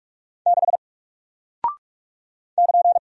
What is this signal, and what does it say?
Morse code, heard as a keyed tone, about 46 wpm, 700 Hz. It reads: 6 <KN>
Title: Single Character (Letter:Number:Punctuation:Prosign) Alt 45wpm